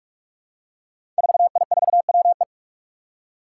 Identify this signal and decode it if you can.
Morse code, heard as a keyed tone, about 42 wpm, 700 Hz. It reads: 4I4WE
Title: Five Challenging Letter-Number Combinations 45wpm